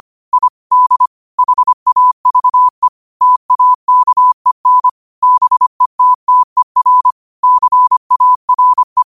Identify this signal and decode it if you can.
Morse code, heard as a keyed tone, about 25 wpm, 1000 Hz. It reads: I D HAVE TAKEN BETTER CARE
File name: 25wpm